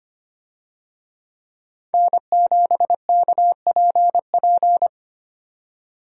Morse code, heard as a keyed tone, about 25 wpm, 700 Hz. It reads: N7KPP
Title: USA Call Signs - 5 Characters - 25wpm